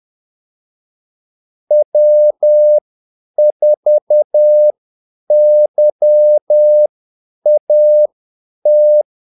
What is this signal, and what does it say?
Morse code, heard as a keyed tone, about 10 wpm, spaced as for 8 wpm, 600 Hz. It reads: W4YAT
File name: Callsigns-10-8wpm